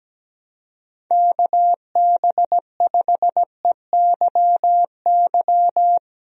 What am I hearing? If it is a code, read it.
Morse code, heard as a keyed tone, about 17 wpm, 700 Hz. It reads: KB5EYY
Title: USA Call Signs - 6 Characters - Repeated 2x - 17wpm